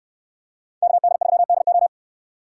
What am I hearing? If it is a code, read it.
Morse code, heard as a keyed tone, about 69 wpm, 700 Hz. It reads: KD3DO